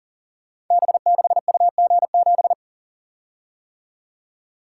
Morse code, heard as a keyed tone, about 42 wpm, 690 Hz. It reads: B6UG7
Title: Code Groups - 5 Letter:Numbers - 40wpm